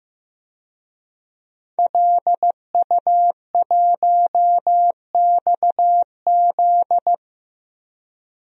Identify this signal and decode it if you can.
Morse code, heard as a keyed tone, about 15 wpm, 700 Hz. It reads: LU1XZ
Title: DXCC-100 Call Signs - 5 Characters - Repeated 2x - 15wpm